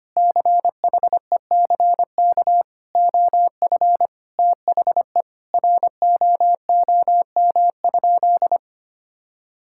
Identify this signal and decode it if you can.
Morse code, heard as a keyed tone, about 25 wpm, 700 Hz. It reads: CHECK OF THE ROOM?
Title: Sentences from Top 1000 Words - Review Part 6 - 25wpm